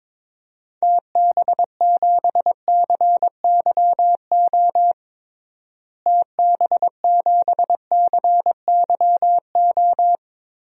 Morse code, heard as a keyed tone, about 21 wpm, 700 Hz. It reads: TB7CYO TB7CYO
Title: DXCC-100 Call Signs - 6 Characters - Repeated 3x - 22wpm